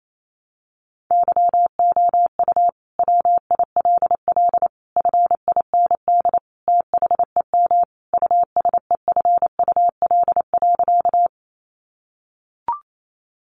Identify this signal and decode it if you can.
Morse code, heard as a keyed tone, about 28 wpm, 700 Hz. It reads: YOU WILL FIND THEM USEFUL.
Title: Sentences from Top 700 Words 28wpm